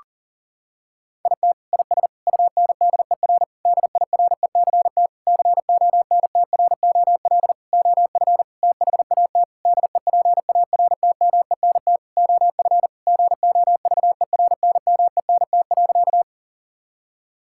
Morse code, heard as a keyed tone, about 37 wpm, 700 Hz. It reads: IT IS UNDER DIRECT CONTROL OF THAT DEPARTMENT OF GOVERNMENT.
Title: Sentences from Top 1000 Words - Review Part 5 - 40wpm